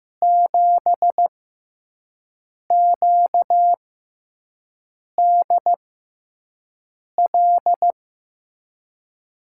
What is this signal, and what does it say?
Morse code, heard as a keyed tone, about 15 wpm, spaced as for 5 wpm, 700 Hz. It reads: 7QDL
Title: USA Call Signs - 6 Characters - 6x Character Spacing - 15wpm